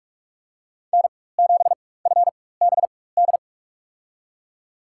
Morse code, heard as a keyed tone, about 47 wpm, spaced as for 21 wpm, 700 Hz. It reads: N7FBD
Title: USA Call Signs - 5 Characters - 4x Character Spacing - 45wpm